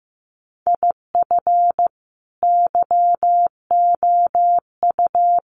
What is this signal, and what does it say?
Morse code, heard as a keyed tone, about 15 wpm, 700 Hz. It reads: IF YOU
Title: Sets of 5 Words 15wpm